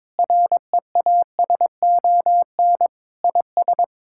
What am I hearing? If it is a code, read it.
Morse code, heard as a keyed tone, about 22 wpm, 700 Hz. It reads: REASON IS
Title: Sentences from Top 1600 Words 22wpm